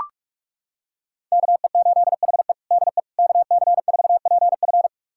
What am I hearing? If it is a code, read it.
Morse code, heard as a keyed tone, about 42 wpm, 700 Hz. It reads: KE8HE DE KC4PF
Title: USA Callsign DE Callsign - 5 Characters - 45wpm